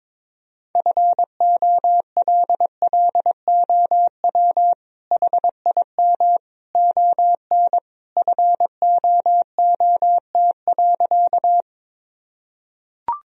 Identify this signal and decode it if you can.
Morse code, heard as a keyed tone, about 22 wpm, 700 Hz. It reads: FOLLOW HIM ON FOOT.
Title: Sentences from Top 1000 Words - Review Part 3 - 22wpm